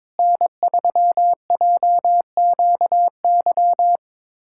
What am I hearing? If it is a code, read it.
Morse code, heard as a keyed tone, about 22 wpm, 700 Hz. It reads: N3JQY
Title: USA Callsign DE Callsign - 5 Characters - 22wpm